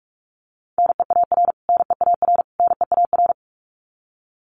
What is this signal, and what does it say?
Morse code, heard as a keyed tone, about 44 wpm, 700 Hz. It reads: NEAR NEAR NEAR
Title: Top 500 Words Repeated 45wpm